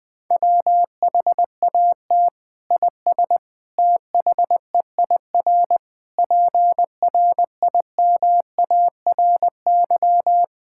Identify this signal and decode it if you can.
Morse code, heard as a keyed tone, about 21 wpm, 700 Hz. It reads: WHAT IS THEIR PRIMARY
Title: Sentences from Top 1000 Words - Review Part 5 - 20wpm